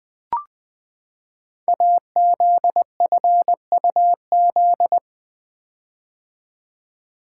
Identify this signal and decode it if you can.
Morse code, heard as a keyed tone, about 20 wpm, 700 Hz. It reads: AZFUZ
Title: Code Groups - 5 Letters - 20wpm